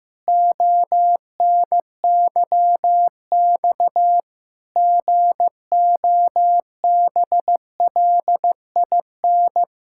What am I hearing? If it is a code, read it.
Morse code, heard as a keyed tone, about 15 wpm, 700 Hz. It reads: ONYX GOBLIN
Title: Pangram Challenge v2 - 15wpm